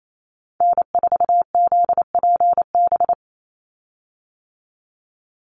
Morse code, heard as a keyed tone, about 28 wpm, 700 Hz. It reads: N4ZPB
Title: USA Call Signs 28wpm